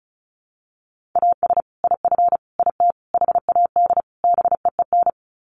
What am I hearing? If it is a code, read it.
Morse code, heard as a keyed tone, about 34 wpm, 700 Hz. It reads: AS IF IT HAD BEEN
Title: Sets of 5 Words Encore2 35wpm